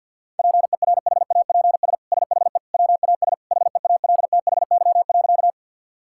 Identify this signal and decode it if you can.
Morse code, heard as a keyed tone, about 50 wpm, 700 Hz. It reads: PERHAPS SHE WAS HEALTHY.
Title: Sentences from Top 1000 Words - Review Part 5 - 50wpm